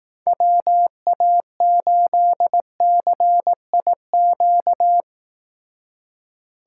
Morse code, heard as a keyed tone, about 18 wpm, 700 Hz. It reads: WA8CIQ
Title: Speed Racing - USA Call Signs 15wpm